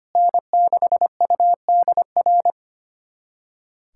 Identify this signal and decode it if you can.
Morse code, heard as a keyed tone, about 25 wpm, 700 Hz. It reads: N6UDR
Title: USA Callsign DE Callsign - 5 Characters - 25wpm